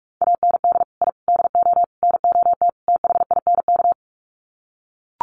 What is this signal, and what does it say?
Morse code, heard as a keyed tone, about 44 wpm, 700 Hz. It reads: AND I DO NOT THINK
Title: Sets of 5 Words 45wpm